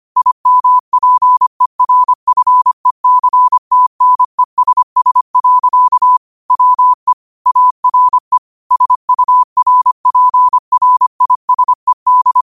Morse code, heard as a keyed tone, about 25 wpm, 1000 Hz. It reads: IMPERFECTNESS. WE ARE SURPRISED